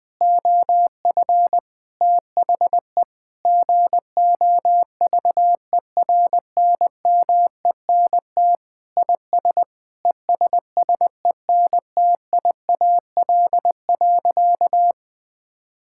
Morse code, heard as a keyed tone, about 20 wpm, 700 Hz. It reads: OF THE GOVERNMENT IS ESSENTIAL.
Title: Sentences with 7-Word Phrases - Spoken Beginning - 20wpm